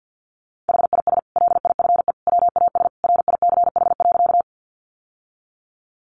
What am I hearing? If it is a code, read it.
Morse code, heard as a keyed tone, about 49 wpm, 700 Hz. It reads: HIS LIFE WAS RICH.
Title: Sentences from Top 1100 Words 50wpm